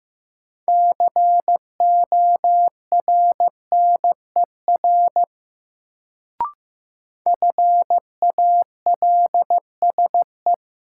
Morse code, heard as a keyed tone, about 15 wpm, 700 Hz. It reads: CORNER FALSE
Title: Sentences from Top 1600 Words - New Word Drill Alt - 15wpm